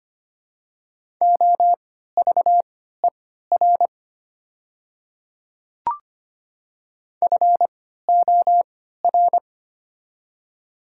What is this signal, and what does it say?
Morse code, heard as a keyed tone, about 25 wpm, spaced as for 14 wpm, 700 Hz. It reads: OVER FOR
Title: 3x Character Spacing - Top 100 Words - 25wpm